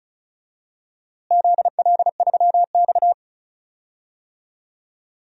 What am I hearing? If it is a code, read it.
Morse code, heard as a keyed tone, about 34 wpm, 700 Hz. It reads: ZL3X
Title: DXCC-100 Call Signs - 4 Characters - Repeated 2x - 35wpm